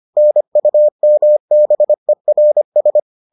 Morse code, heard as a keyed tone, about 25 wpm, 600 Hz. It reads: NUMBERS